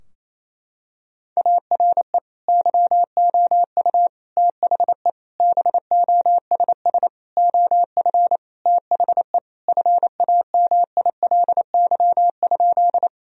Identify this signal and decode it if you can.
Morse code, heard as a keyed tone, about 28 wpm, 700 Hz. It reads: ARE YOU THE BOSS OF THE FAMILY?
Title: Sentences from Top 1000 Words - Review Part 4 - 28wpm